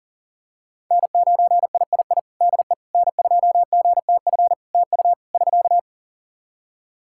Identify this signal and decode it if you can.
Morse code, heard as a keyed tone, about 40 wpm, 700 Hz. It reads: N9III DE N2GTF TU <SK>
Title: Practice QSO - Exchange 4 - Ending - 40wpm